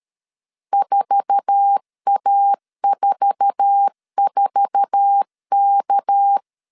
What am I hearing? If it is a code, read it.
Morse code, heard as a keyed tone, about 13 wpm, 800 Hz. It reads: 4A44K
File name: LEZIONI / 8 YJ45 / 8 YJ4 12wpm 2151